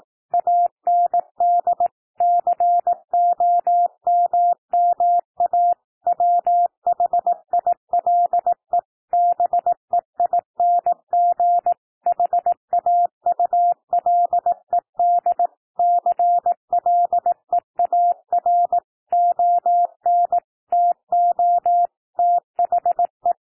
Morse code, heard as a keyed tone, about 18 wpm, 700 Hz. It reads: AND COMMA WHILE BEING HAULED CLEAR ON TO THE
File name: TEXT PASSAGE 18 wpm